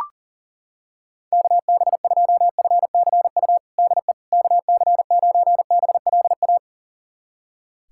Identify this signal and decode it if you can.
Morse code, heard as a keyed tone, about 37 wpm, 700 Hz. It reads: KB2FCU DE KC9BLA
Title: USA Callsign DE Callsign 40wpm